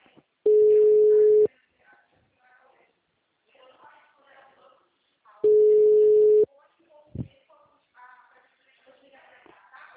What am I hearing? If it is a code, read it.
Ringback tone (Europe, 425 Hz), with speech.
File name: out- -2002- - - .51924